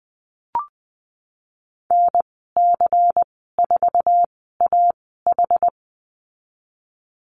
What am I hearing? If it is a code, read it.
Morse code, heard as a keyed tone, about 20 wpm, spaced as for 15 wpm, 700 Hz. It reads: NC4AH
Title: 186 Lesson 30g - Letter b - Call Signs 20wpm